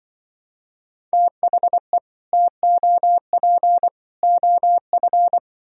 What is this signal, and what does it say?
Morse code, heard as a keyed tone, about 24 wpm, 700 Hz. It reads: THE TOP OF
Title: Speed-Racing - Sets of 3 Words 20wpm